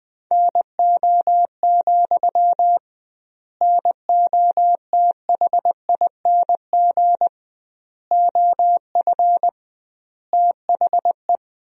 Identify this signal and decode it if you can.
Morse code, heard as a keyed tone, about 20 wpm, 700 Hz. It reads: NO, NOTHING OF THE
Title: 2x Word-Spacing - Sentences from Top 500 Words - 20wpm